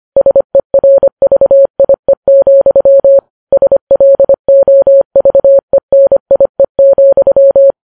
Morse code, heard as a keyed tone, about 25 wpm, 550 Hz. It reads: SERVIE, SLOVENIE,